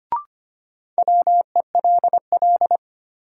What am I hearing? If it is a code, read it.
Morse code, heard as a keyed tone, about 25 wpm, 700 Hz. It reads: WELL
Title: Sets of 2 Words - New Word Drill - 25wpm